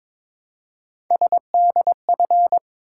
Morse code, heard as a keyed tone, about 22 wpm, 700 Hz. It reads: SDF
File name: practice / 22wpm / Three Letter-Number Combinations - 22wpm